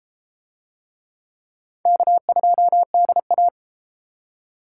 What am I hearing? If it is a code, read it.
Morse code, heard as a keyed tone, about 31 wpm, 700 Hz. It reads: K2DA